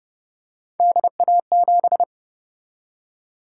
Morse code, heard as a keyed tone, about 30 wpm, 700 Hz. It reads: DA7